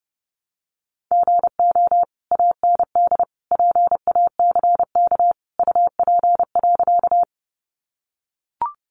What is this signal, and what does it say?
Morse code, heard as a keyed tone, about 30 wpm, 700 Hz. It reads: GO AND PACK UP.
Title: Sentences from Top 1300 Words 30wpm